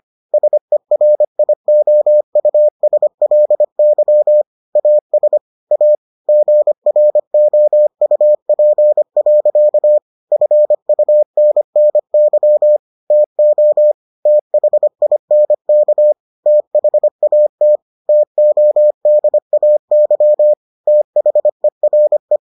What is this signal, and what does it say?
Morse code, heard as a keyed tone, about 25 wpm, 600 Hz. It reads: SERIOUSLY AS A GROUP. FUNNY TO THINK THAT TODAY THERE